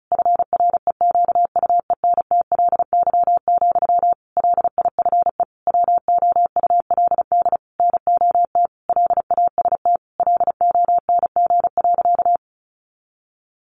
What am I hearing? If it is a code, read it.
Morse code, heard as a keyed tone, about 34 wpm, 700 Hz. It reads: FREQUENTLY, LIFE WOULD NOT LAST LONG.
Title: Sentences from Top 1300 Words 35wpm